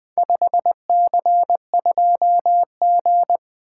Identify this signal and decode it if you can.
Morse code, heard as a keyed tone, about 20 wpm, 700 Hz. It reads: 5C2G
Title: Four Letter-Number Combinations - 20wpm